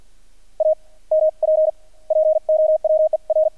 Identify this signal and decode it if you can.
Morse code, heard as a keyed tone, about 48 wpm, spaced as for 35 wpm, 620 Hz. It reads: AMP POWER
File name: archived / CRSnet